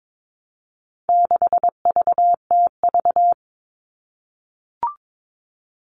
Morse code, heard as a keyed tone, about 22 wpm, 700 Hz. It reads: 6VTV
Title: All Cut Numbers - 4-Digit 22wpm